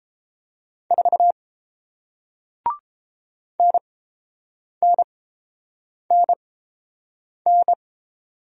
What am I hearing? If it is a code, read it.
Morse code, heard as a keyed tone, about 29 wpm, spaced as for 7 wpm, 700 Hz. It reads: 4 NNNN